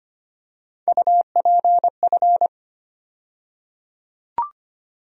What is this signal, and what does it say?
Morse code, heard as a keyed tone, about 25 wpm, 700 Hz. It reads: UPF
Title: USA Call Sign Postfixes - 3 Letters 25wpm